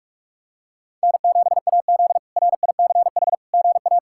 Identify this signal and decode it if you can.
Morse code, heard as a keyed tone, about 46 wpm, 700 Hz. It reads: N7AZ RICH GA
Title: K1USN SST QSO Practice - 45wpm